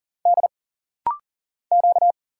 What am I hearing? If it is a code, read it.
Morse code, heard as a keyed tone, about 40 wpm, 700 Hz. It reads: D Q
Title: Single Letters - Rapid-Fire 40wpm